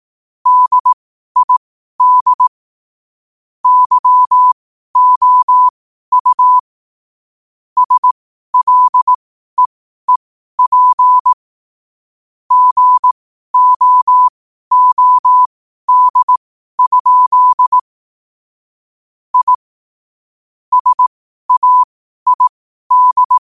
Morse code, heard as a keyed tone, about 18 wpm, spaced as for 12 wpm, 1000 Hz. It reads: DID YOU SLEEP GOOD? I SAID
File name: qotdfiles / 12wpm